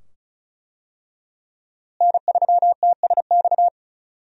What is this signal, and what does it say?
Morse code, heard as a keyed tone, about 36 wpm, 700 Hz. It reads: N3TSX
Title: USA Call Signs - 5 Characters - Repeated 2x - 35wpm